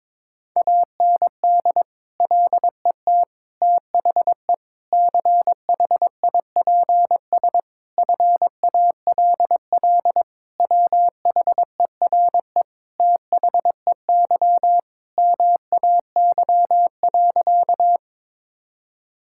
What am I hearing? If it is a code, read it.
Morse code, heard as a keyed tone, about 22 wpm, 700 Hz. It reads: AND LET THE CHIPS FALL WHERE THEY MAY.
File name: practice / 22wpm / Sentences with 8-Word Phrases - 22wpm